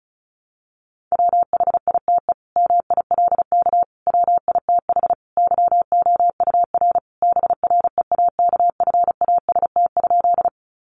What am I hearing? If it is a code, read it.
Morse code, heard as a keyed tone, about 34 wpm, 700 Hz. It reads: WHITE MILK WITH YOUR BREAKFAST?
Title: Sentences from Top 1500 Words 35wpm